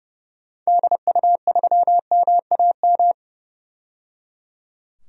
Morse code, heard as a keyed tone, about 30 wpm, 700 Hz. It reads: DU3MAM
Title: DXCC-100 Call Signs - 6 Characters - 30wpm